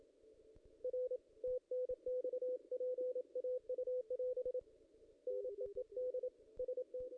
Morse code, heard as a keyed tone, about 28 wpm, 500 Hz. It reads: R TNX PAU <AS> NEED ST
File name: contests / ss-cw-k5tr-op-n5rz -2200